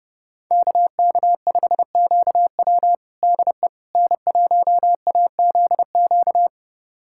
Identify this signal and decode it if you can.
Morse code, heard as a keyed tone, about 30 wpm, 700 Hz. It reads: KK5QW DE N1AZQ
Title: USA Callsign DE Callsign - 5 Characters - 30wpm